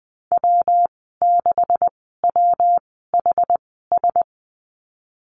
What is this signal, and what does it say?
Morse code, heard as a keyed tone, about 20 wpm, spaced as for 15 wpm, 700 Hz. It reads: W6WHS
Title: 245 Lesson 39g - Letter z - Call Signs 20wpm